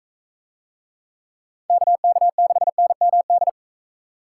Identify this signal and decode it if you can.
Morse code, heard as a keyed tone, about 38 wpm, 700 Hz. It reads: KK6NMD